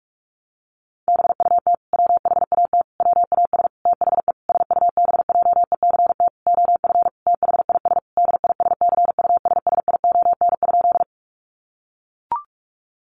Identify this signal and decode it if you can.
Morse code, heard as a keyed tone, about 44 wpm, 700 Hz. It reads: BUT WHAT WAS THE SUBJECT OF THIS DISCUSSION?
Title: Sentences from Top 1100 Words 45wpm